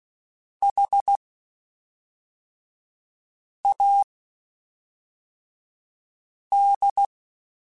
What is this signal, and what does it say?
Morse code, heard as a keyed tone, about 16 wpm, spaced as for 3 wpm, 800 Hz. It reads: HAD
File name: Spaced Characters / Spaced Characters-mp3s / Weather5 5 sp @5